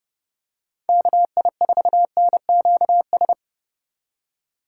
Morse code, heard as a keyed tone, about 30 wpm, 700 Hz. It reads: KI4NQS
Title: USA Call Signs - 6 Characters - 3x Character Spacing - 30wpm